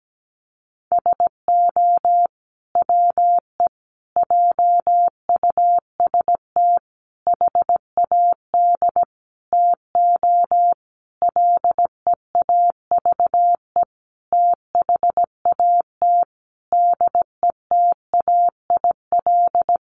Morse code, heard as a keyed tone, about 17 wpm, 700 Hz. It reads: SO WE JUST HAD TO LEAVE THAT DETAIL
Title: Sentences from Top 1500 Words 17wpm